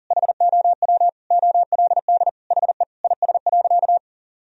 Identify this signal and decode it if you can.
Morse code, heard as a keyed tone, about 40 wpm, 700 Hz. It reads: HOW OLD HE IS.
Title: Sentences from Top 1000 Words - Review Part 4 - 40wpm